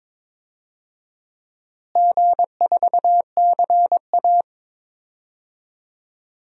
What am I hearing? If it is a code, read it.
Morse code, heard as a keyed tone, about 22 wpm, 700 Hz. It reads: G4CA
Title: DXCC-100 Call Signs - 4 Characters - Repeated 2x - 22wpm